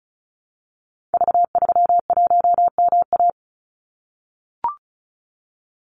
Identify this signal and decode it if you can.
Morse code, heard as a keyed tone, about 36 wpm, 700 Hz. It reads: V31MA
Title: Call Signs - 35wpm